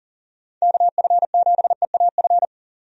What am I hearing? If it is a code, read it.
Morse code, heard as a keyed tone, about 40 wpm, 700 Hz. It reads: KF7EAF
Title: USA Call Signs - 6 Characters - 40wpm